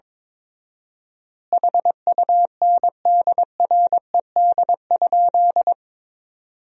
Morse code, heard as a keyed tone, about 22 wpm, 700 Hz. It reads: HUNDRED?
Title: 4x Word-Spacing - Sentences from Top 400 Words - 22wpm